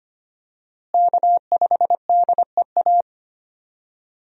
Morse code, heard as a keyed tone, about 25 wpm, 700 Hz. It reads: K5DEA
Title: USA Call Signs - 5 Characters - Repeated 2x - 25wpm